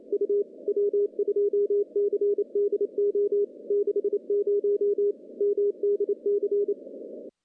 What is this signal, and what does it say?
Morse code, heard as a keyed tone, about 30 wpm, 410 Hz. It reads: U W2CDO 60 MDC
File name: contests / CW-SS-k5tr- -1400